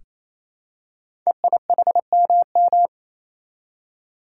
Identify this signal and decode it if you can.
Morse code, heard as a keyed tone, about 27 wpm, 700 Hz. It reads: EIHMM
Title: Code Groups - 5 Letters - 28wpm